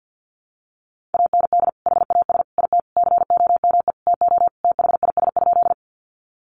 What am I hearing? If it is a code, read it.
Morse code, heard as a keyed tone, about 50 wpm, 700 Hz. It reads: AND HAS IT COME TO THIS?
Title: Sentences from Top 100 Words 50wpm